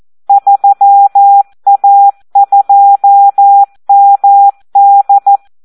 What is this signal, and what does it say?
Morse code, heard as a keyed tone, about 14 wpm, 800 Hz. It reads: 3A2MD
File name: Audio practica CW / SD 1 14